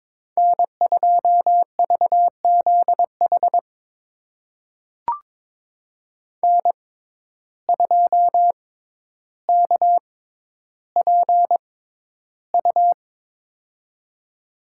Morse code, heard as a keyed tone, about 22 wpm, spaced as for 11 wpm, 700 Hz. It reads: N2VZH N2KPU
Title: USA Call Signs - 6x Character Spacing - 22wpm